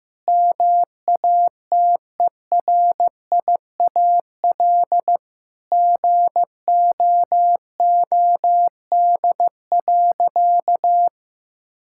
Morse code, heard as a keyed tone, about 15 wpm, 700 Hz. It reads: MATERIAL GOOD.
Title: Sentences from Top 1000 Words - Review Part 5 - 15wpm